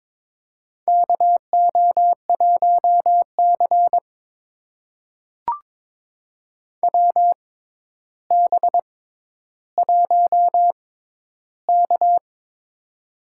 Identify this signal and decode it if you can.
Morse code, heard as a keyed tone, about 22 wpm, spaced as for 11 wpm, 700 Hz. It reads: KO1C WB1K
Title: USA Call Signs - 4 Characters - 6x Character Spacing - 22wpm